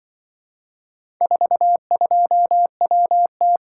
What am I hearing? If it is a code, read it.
Morse code, heard as a keyed tone, about 24 wpm, 700 Hz. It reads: 42WT
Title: Speed-Racing - Four Letter-Number Combinations - 20wpm